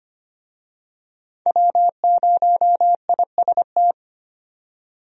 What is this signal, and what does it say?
Morse code, heard as a keyed tone, about 25 wpm, 700 Hz. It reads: W0IST
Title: USA Call Signs - 5 Characters - 25wpm